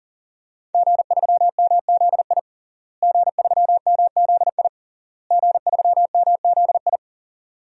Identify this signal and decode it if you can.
Morse code, heard as a keyed tone, about 37 wpm, 700 Hz. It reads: G3MZI G3MZI G3MZI
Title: DXCC-100 Call Signs - 5 Characters - Repeated 3x - 40wpm